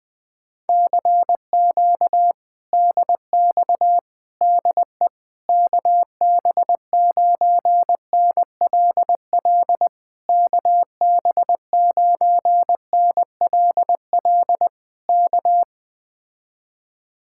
Morse code, heard as a keyed tone, about 20 wpm, 700 Hz. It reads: CQ DX DE KB9NLL KB9NLL K